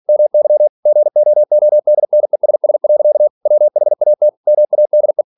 Morse code, heard as a keyed tone, about 46 wpm, 600 Hz. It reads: MY GOODNESS. WHAT MADE
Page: Never 47wpm